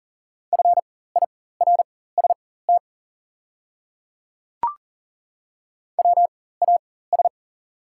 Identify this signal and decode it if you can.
Morse code, heard as a keyed tone, about 40 wpm, spaced as for 19 wpm, 700 Hz. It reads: FIRST WAS
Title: 4x Character Spacing - Top 100 Words - 40wpm